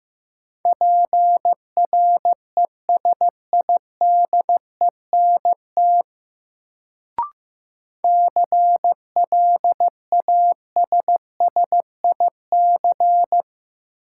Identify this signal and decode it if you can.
Morse code, heard as a keyed tone, about 16 wpm, 700 Hz. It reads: PRESIDENT CLASSIC
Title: Sentences from Top 1000 Words - New Word Drill - 15wpm